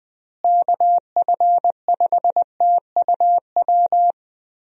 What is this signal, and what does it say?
Morse code, heard as a keyed tone, about 20 wpm, 700 Hz. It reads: KF5TUW
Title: USA Call Signs - 6 Characters - Repeated 3x - 20wpm